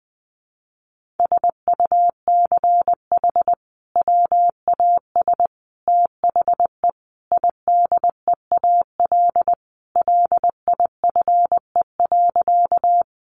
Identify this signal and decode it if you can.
Morse code, heard as a keyed tone, about 20 wpm, 700 Hz. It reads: SUCH WAS THE IDEAL LIFE.
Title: Sentences from Top 1400 Words 20wpm